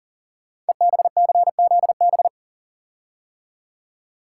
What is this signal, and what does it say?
Morse code, heard as a keyed tone, about 42 wpm, 700 Hz. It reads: EBCZB
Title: Code Groups - 5 Letters - 40wpm